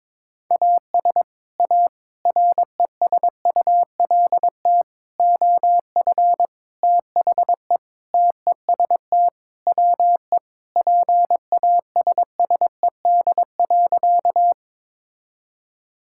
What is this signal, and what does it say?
Morse code, heard as a keyed tone, about 22 wpm, 700 Hz. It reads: AS A RESULT OF THE TEST WE PASSED.
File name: practice / 22wpm / Sentences with 4-Word Phrases - 22wpm